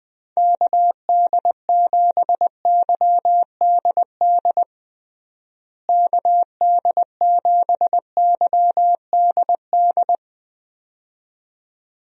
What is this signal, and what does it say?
Morse code, heard as a keyed tone, about 21 wpm, 700 Hz. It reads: KD7YDD KD7YDD
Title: USA Call Signs - 6 Characters - Repeated 2x - 20wpm